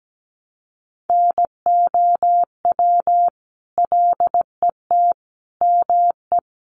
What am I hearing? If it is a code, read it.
Morse code, heard as a keyed tone, about 17 wpm, 700 Hz. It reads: NOW LET ME
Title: Sentences from Top 1200 Words 17wpm